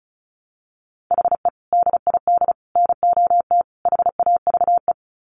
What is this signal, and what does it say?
Morse code, heard as a keyed tone, about 34 wpm, 700 Hz. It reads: HE DID NOT HAVE
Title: Sets of 4 Words 35wpm